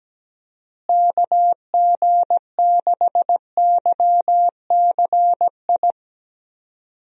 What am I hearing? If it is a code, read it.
Morse code, heard as a keyed tone, about 17 wpm, 700 Hz. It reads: KG6YCI
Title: USA Call Signs - 6 Characters - Repeated 3x - 17wpm